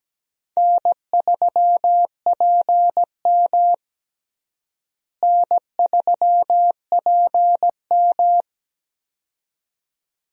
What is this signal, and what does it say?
Morse code, heard as a keyed tone, about 17 wpm, 700 Hz. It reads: N3PM N3PM
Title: USA Call Signs - 4 Characters - Repeated 2x - 17wpm